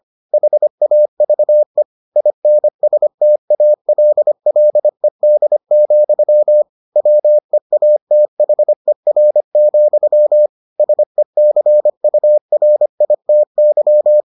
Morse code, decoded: HAVE INSTALLED, WEATHER, SECURITY

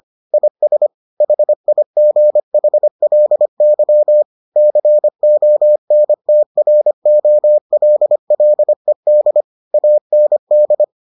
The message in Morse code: IS HIGHLY CONTROLLED AND

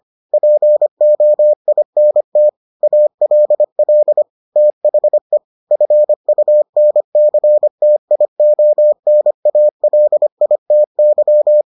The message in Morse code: POINT ALL THE FUNCTIONALITY

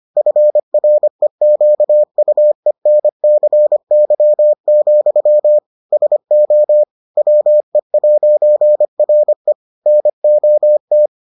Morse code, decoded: FREQUENCY, SO WE'RE NOT

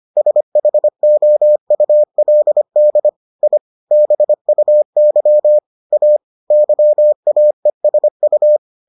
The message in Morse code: SHOULD I BUY A YAESU

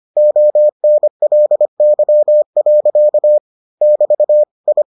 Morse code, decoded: ONLY. <BT> I

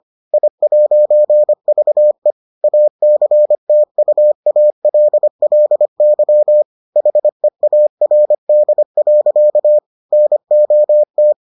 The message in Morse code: I'VE ACTUALLY HEARD. NOT

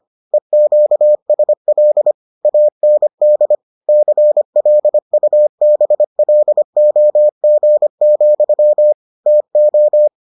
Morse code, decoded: EQSL AND CLUBLOG, TO

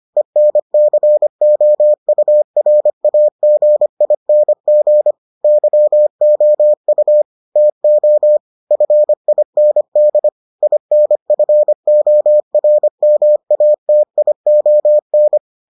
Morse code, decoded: ENCOURAGING YOU TO FIND INFORMATION

